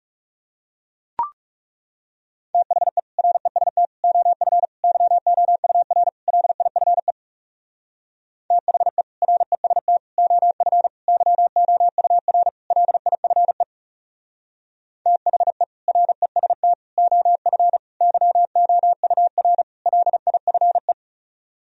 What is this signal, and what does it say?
Morse code, heard as a keyed tone, about 40 wpm, 700 Hz. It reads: THE REST OF YOUR LIFE THE REST OF YOUR LIFE THE REST OF YOUR LIFE